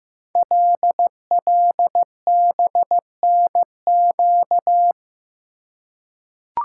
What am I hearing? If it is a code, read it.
Morse code, heard as a keyed tone, about 15 wpm, 700 Hz. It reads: LLBNQ
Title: Code Groups - 5 Letters - 15wpm